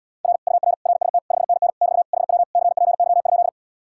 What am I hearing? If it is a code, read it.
Morse code, heard as a keyed tone, about 80 wpm, 700 Hz. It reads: R UR RST 5NN <BT> HW CPY?